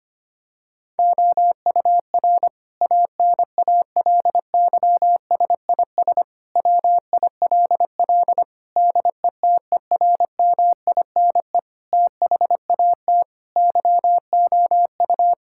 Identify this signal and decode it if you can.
Morse code, heard as a keyed tone, about 25 wpm, 700 Hz. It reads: OUR ANALYSIS WILL DETERMINE THAT YOU
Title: Sentences from Top 1000 Words - Review Part 5 - 25wpm